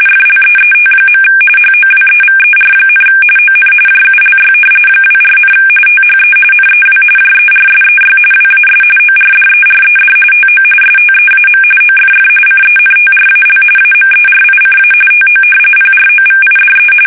RTTY, about 45.45 baud. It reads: FLEET. HOWEVER, ON JULY 24, THE NEW JERSEY WAS ORDERED TO STEAM OFF CENTRAL AMERICA TO PROVIDE
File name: fsk850 / battleshipnj